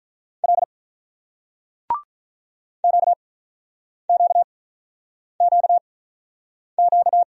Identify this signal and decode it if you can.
Morse code, heard as a keyed tone, about 49 wpm, spaced as for 9 wpm, 690 Hz. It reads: L QQQQ